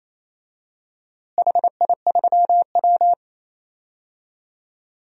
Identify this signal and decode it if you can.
Morse code, heard as a keyed tone, about 28 wpm, 700 Hz. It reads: HI3W
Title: DXCC-100 Call Signs - 4 Characters - Repeated 2x - 28wpm